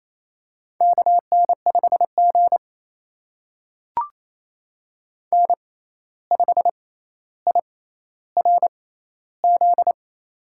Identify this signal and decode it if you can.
Morse code, heard as a keyed tone, about 27 wpm, spaced as for 13 wpm, 700 Hz. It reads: KN5G N5IRZ
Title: USA Call Signs - 6x Character Spacing - 28wpm